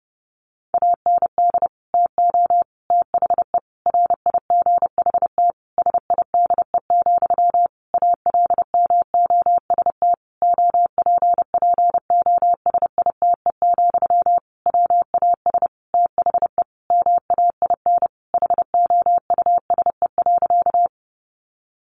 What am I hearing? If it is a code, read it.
Morse code, heard as a keyed tone, about 30 wpm, 700 Hz. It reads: AND TO THE RIGHT SIDE, ALMOST OPPOSITE, WAS THE MAIN HOUSE.
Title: Sentences from Top 1300 Words 30wpm